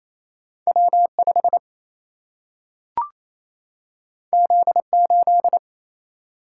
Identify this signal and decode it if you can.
Morse code, heard as a keyed tone, about 28 wpm, 700 Hz. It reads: W5 Z8
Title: Single-Letter Single-Number 28wpm